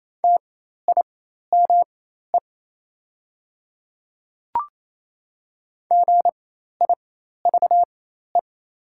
Morse code, heard as a keyed tone, about 28 wpm, spaced as for 13 wpm, 700 Hz. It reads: TIME GIVE
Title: 4x Character Spacing - Top 200 Words - 28wpm